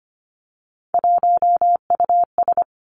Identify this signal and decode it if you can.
Morse code, heard as a keyed tone, about 25 wpm, 700 Hz. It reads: 1US
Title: Three Challenging Letter-Number Combinations 25wpm